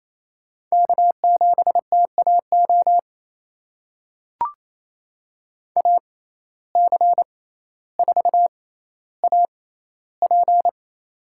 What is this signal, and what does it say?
Morse code, heard as a keyed tone, about 28 wpm, spaced as for 14 wpm, 700 Hz. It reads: K7TAO AC4AP
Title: USA Call Signs - 5 Characters - 6x Character Spacing - 28wpm